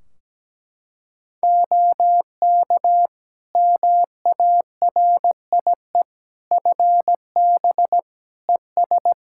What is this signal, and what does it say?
Morse code, heard as a keyed tone, about 17 wpm, 700 Hz. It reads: OK MARIE FB ES
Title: Practice QSO - Exchange 2 - Info - 17wpm